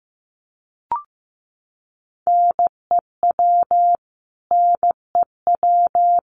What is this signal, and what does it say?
Morse code, heard as a keyed tone, about 15 wpm, 700 Hz. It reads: NEW NEW
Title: Top 500 Words Repeated 15wpm